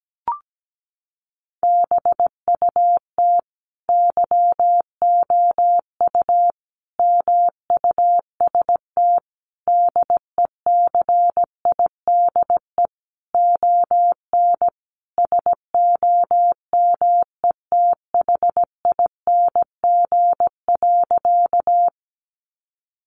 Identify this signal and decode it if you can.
Morse code, heard as a keyed tone, about 17 wpm, 700 Hz. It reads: BUT YOU MUST DECIDE ON SOMETHING.
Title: Sentences from Top 600 Words 17wpm